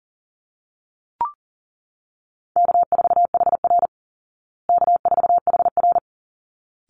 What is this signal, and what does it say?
Morse code, heard as a keyed tone, about 40 wpm, 700 Hz. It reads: K4HR K4HR
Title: Call Signs Repeated 3x - 40wpm